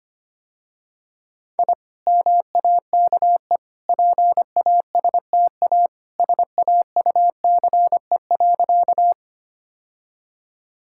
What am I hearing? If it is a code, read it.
Morse code, heard as a keyed tone, about 25 wpm, 690 Hz. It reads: I MAKE PASTA SAUCE.
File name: Easy Sentences with Hints - 25wpm